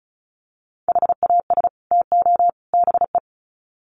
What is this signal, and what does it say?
Morse code, heard as a keyed tone, about 34 wpm, 700 Hz. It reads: HAS TO BE
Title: Sets of 3 Words 35wpm